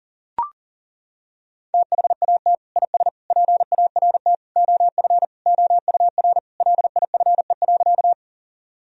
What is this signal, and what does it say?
Morse code, heard as a keyed tone, about 37 wpm, 700 Hz. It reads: THAT IS PART OF OUR LIFE.
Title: Sentences from Top 1000 Words - Review Part 1 - 40wpm